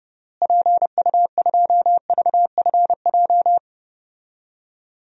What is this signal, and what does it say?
Morse code, heard as a keyed tone, about 30 wpm, 700 Hz. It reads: PU2VFJ
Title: DXCC-100 Call Signs - 6 Characters - Repeated 2x - 30wpm